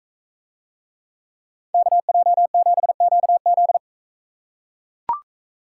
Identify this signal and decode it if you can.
Morse code, heard as a keyed tone, about 42 wpm, 700 Hz. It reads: KJ7QZ